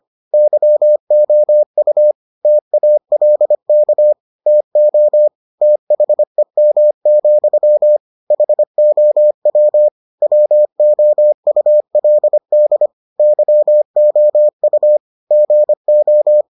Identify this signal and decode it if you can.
Morse code, heard as a keyed tone, about 25 wpm, 600 Hz. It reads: YOU TALK TO THEM, HOW WOULD YOU GO